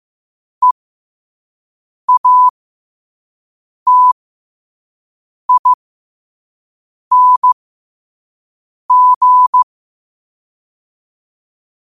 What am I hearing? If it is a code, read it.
Morse code, heard as a keyed tone, about 15 wpm, spaced as for 5 wpm, 1000 Hz. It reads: EATING
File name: qotdfiles / 5wpm